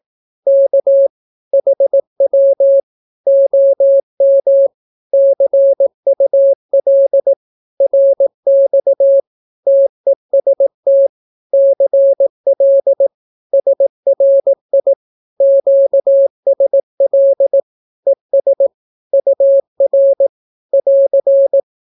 Morse code, heard as a keyed tone, about 18 wpm, 550 Hz. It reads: K HW OM CUL RX TEST CL SRI QSL ES UR <AR>
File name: naukapodstaw / skroty / skroty